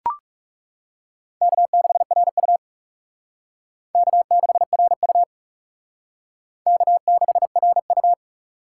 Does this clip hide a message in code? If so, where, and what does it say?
Yes: Morse code from 1.4 to 8.1 s, heard as a keyed tone, about 38 wpm, 700 Hz. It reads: K6RU K6RU K6RU